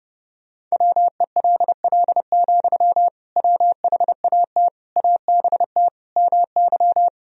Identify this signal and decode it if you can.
Morse code, heard as a keyed tone, about 30 wpm, 700 Hz. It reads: WELL, WHAT ABT MY
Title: Sentences from Top 500 Words and Abbreviations 30wpm